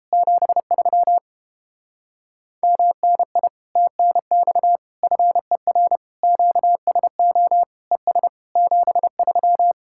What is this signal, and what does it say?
Morse code, heard as a keyed tone, about 31 wpm, 700 Hz. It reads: 73 MNI TNX FER QSO ES 73